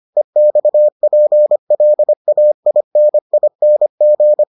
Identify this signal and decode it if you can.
Morse code, heard as a keyed tone, about 25 wpm, 600 Hz. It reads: EXPLAINING